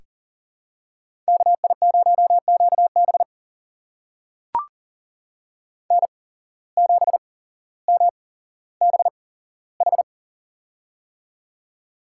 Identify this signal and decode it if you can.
Morse code, heard as a keyed tone, about 40 wpm, spaced as for 17 wpm, 700 Hz. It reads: KI0QB N7MBH
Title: USA Call Signs - 5 Characters - 8x Character Spacing - 40wpm